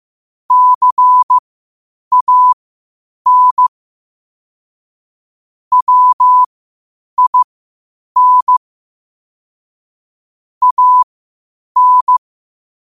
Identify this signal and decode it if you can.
Morse code, heard as a keyed tone, about 15 wpm, spaced as for 8 wpm, 1000 Hz. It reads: CAN WIN AN
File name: 8wpm